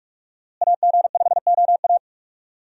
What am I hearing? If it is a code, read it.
Morse code, heard as a keyed tone, about 46 wpm, 700 Hz. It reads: AG5OA
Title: USA Call Signs - 5 Characters - Repeated 2x - 45wpm